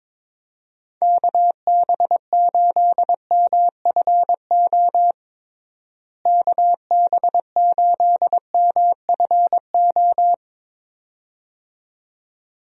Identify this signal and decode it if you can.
Morse code, heard as a keyed tone, about 22 wpm, 700 Hz. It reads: KB8MFO KB8MFO
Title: USA Call Signs - 6 Characters - Repeated 2x - 22wpm